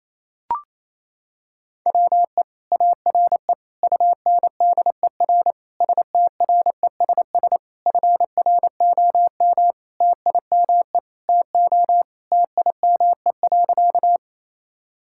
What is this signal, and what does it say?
Morse code, heard as a keyed tone, about 28 wpm, 700 Hz. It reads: WE ARE UNDER STRESS FROM TIME TO TIME.
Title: Sentences from Top 1000 Words - Review Part 6 - 28wpm